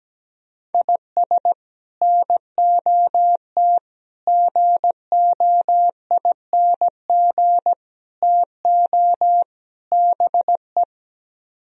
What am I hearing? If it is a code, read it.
Morse code, heard as a keyed tone, about 17 wpm, 700 Hz. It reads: IS NOT GOING TO BE
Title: Speed-Racing - Sets of 5 Words 17wpm